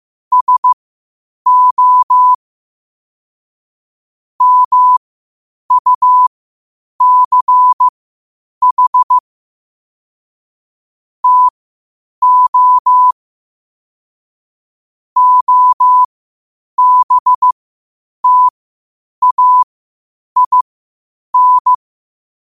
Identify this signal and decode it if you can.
Morse code, heard as a keyed tone, about 15 wpm, spaced as for 8 wpm, 1000 Hz. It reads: SO MUCH TO OBTAIN
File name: qotdfiles / 8wpm